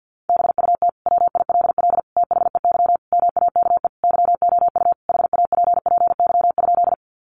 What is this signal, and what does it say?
Morse code, heard as a keyed tone, about 50 wpm, 700 Hz. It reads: BUT WILL THEY MAKE YOU HAPPY?
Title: Sentences from Top 1000 Words 50wpm